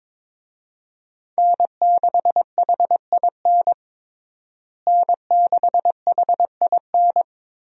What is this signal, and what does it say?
Morse code, heard as a keyed tone, about 22 wpm, 700 Hz. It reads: N6HIN N6HIN
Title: USA Call Signs - 5 Characters - Repeated 2x - 22wpm